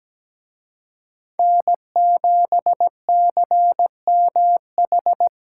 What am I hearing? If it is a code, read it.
Morse code, heard as a keyed tone, about 20 wpm, 700 Hz. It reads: N7CMH
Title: Speed Racing - USA 1x3 Call Signs 17wpm